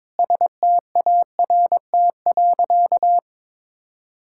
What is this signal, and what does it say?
Morse code, heard as a keyed tone, about 22 wpm, 700 Hz. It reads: START.
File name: Sentences with 7-Word Phrases - 22wpm